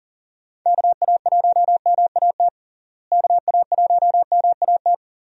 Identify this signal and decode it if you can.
Morse code, heard as a keyed tone, about 40 wpm, 700 Hz. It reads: KA1MAT KA1MAT
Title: USA Call Signs - 6 Characters - Repeated 3x - 40wpm